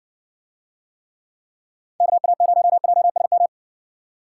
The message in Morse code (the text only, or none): KA0JSM